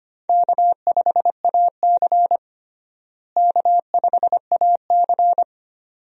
Morse code, heard as a keyed tone, about 25 wpm, 700 Hz. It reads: K5AC K5AC
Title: USA Call Signs - 4 Characters - Repeated 2x - 25wpm